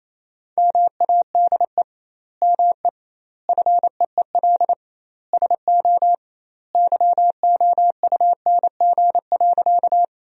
Morse code, heard as a keyed tone, about 28 wpm, 700 Hz. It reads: MADE ME FEEL SO YOUNG.
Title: 2x Word-Spacing - Sentences from Top 400 Words - 28wpm